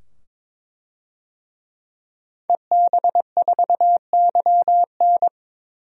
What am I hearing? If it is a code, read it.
Morse code, heard as a keyed tone, about 22 wpm, 700 Hz. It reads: EB4YN
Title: DXCC-100 Call Signs - 5 Characters - 22wpm